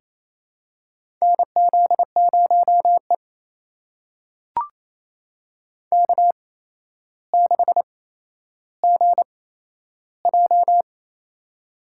Morse code, heard as a keyed tone, about 28 wpm, spaced as for 12 wpm, 700 Hz. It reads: NZ0E K6GJ
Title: USA Call Signs - 4 Characters - 8x Character Spacing - 28wpm